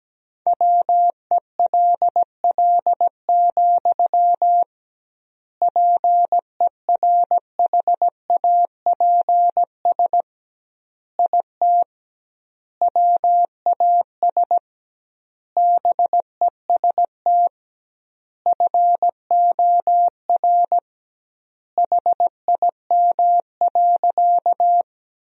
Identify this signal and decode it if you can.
Morse code, heard as a keyed tone, about 17 wpm, 700 Hz. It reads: WELL, PERHAPS IT WAS BEST FOR HIM.
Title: 2x Word-Spacing - Sentences from Top 500 Words - 17wpm